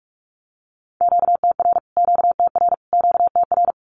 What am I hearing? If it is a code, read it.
Morse code, heard as a keyed tone, about 44 wpm, 700 Hz. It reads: QTR QTR QTR
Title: Q-Codes Repeated 3x 45wpm